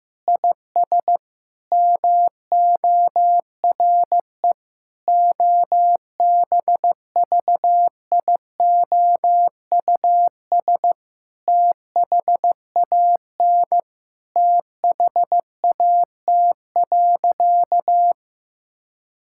Morse code, heard as a keyed tone, about 15 wpm, 700 Hz. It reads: IS MORE OBVIOUS THAN THAT.
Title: Sentences from Top 1600 Words 15wpm